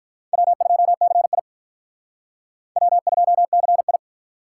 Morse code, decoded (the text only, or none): W2CI W2CI